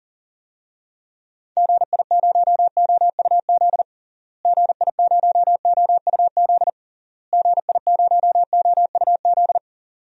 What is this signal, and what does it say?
Morse code, heard as a keyed tone, about 37 wpm, 700 Hz. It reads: GI0OUZ GI0OUZ GI0OUZ
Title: DXCC-100 Call Signs - 6 Characters - Repeated 3x - 40wpm